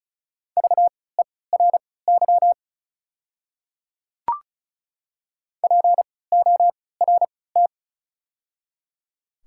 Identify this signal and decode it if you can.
Morse code, heard as a keyed tone, about 34 wpm, spaced as for 20 wpm, 700 Hz. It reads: VERY PORT
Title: 3x Character Spacing - Top 300 Words - 35wpm